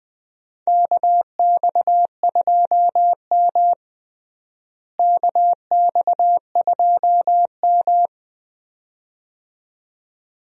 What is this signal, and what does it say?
Morse code, heard as a keyed tone, about 20 wpm, 700 Hz. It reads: KX2M KX2M
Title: USA Call Signs - 4 Characters - Repeated 2x - 20wpm